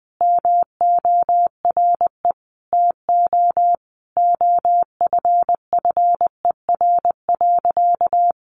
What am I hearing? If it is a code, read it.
Morse code, heard as a keyed tone, about 20 wpm, 700 Hz. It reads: MORE TO OFFER.
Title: Sentences from Top 1100 Words 20wpm